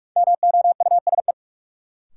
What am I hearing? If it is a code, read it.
Morse code, heard as a keyed tone, about 46 wpm, 700 Hz. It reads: MOUSE
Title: Sentences from Top 700 Words - New Word Drill - 45wpm